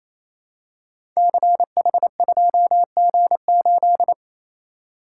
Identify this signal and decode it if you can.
Morse code, heard as a keyed tone, about 28 wpm, 700 Hz. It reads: CH2G8
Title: Code Groups - 5 Letter:Numbers - 28wpm